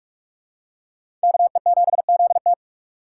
Morse code, heard as a keyed tone, about 41 wpm, 700 Hz. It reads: KE7ZT